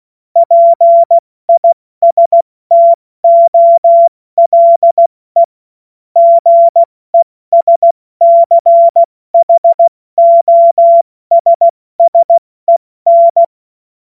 Morse code, heard as a keyed tone, about 16 wpm, spaced as for 14 wpm, 680 Hz. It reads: PISTOLE GESCHOSSEN
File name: Morsekurs / Witze / Witz 16. August 70 F80